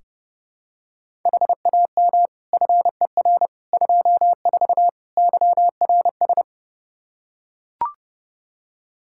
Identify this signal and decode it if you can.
Morse code, heard as a keyed tone, about 32 wpm, 700 Hz. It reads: HAM FER 24 YRS
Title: Practice QSO - Exchange 3 - More Info - 30wpm